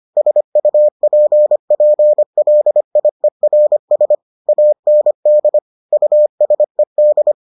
Morse code, heard as a keyed tone, about 25 wpm, 600 Hz. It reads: SUPPLIERS AND USED